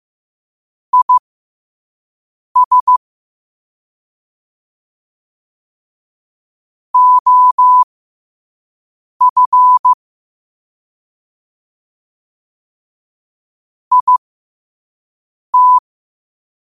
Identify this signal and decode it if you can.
Morse code, heard as a keyed tone, about 15 wpm, spaced as for 5 wpm, 1000 Hz. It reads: IS OF IT
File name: qotdfiles / 5wpm